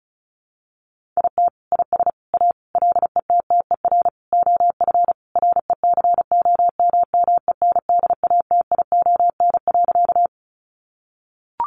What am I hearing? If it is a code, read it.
Morse code, heard as a keyed tone, about 34 wpm, 700 Hz. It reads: IT IS A LETTER OF RECOMMENDATION.
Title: Sentences from Top 1400 Words 35wpm